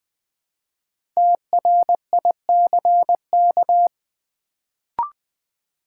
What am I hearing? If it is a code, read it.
Morse code, heard as a keyed tone, about 20 wpm, 700 Hz. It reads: TRICK
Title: Build A Word - 1 Letter At A Time - 5-Letter Words - 20wpm